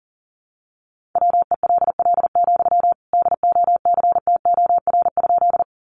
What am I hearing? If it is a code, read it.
Morse code, heard as a keyed tone, about 40 wpm, 700 Hz. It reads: WELL, DOCTOR?
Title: Sentences from Top 1500 Words 40wpm